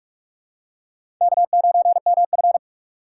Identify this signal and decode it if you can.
Morse code, heard as a keyed tone, about 42 wpm, 700 Hz. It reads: K9MF